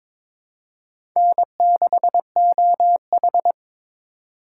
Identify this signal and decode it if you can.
Morse code, heard as a keyed tone, about 22 wpm, 700 Hz. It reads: N6OH
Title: USA Call Signs - 4 Characters - 3x Character Spacing - 22wpm